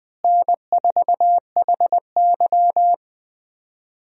Morse code, heard as a keyed tone, about 21 wpm, 700 Hz. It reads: N4HY
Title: USA Call Signs - 4 Characters - Repeated 2x - 20wpm